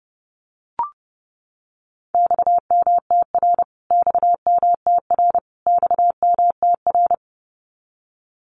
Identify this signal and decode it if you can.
Morse code, heard as a keyed tone, about 30 wpm, 700 Hz. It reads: XMTR XMTR XMTR
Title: Abbreviations Repeated 3x - 30wpm